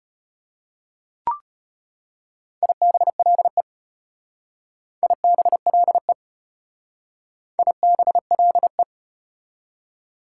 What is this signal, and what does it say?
Morse code, heard as a keyed tone, about 34 wpm, 700 Hz. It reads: IBLE IBLE IBLE